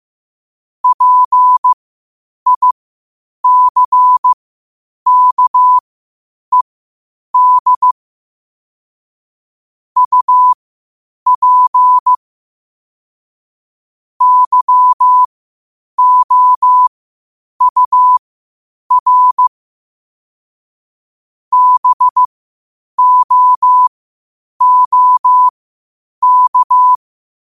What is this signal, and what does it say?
Morse code, heard as a keyed tone, about 15 wpm, spaced as for 8 wpm, 1000 Hz. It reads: PICKED UP YOUR BOOK
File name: qotdfiles / 8wpm